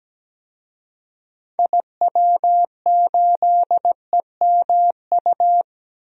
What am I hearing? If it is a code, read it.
Morse code, heard as a keyed tone, about 17 wpm, 700 Hz. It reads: IW8EMU
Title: DXCC-100 Call Signs - 6 Characters - 17wpm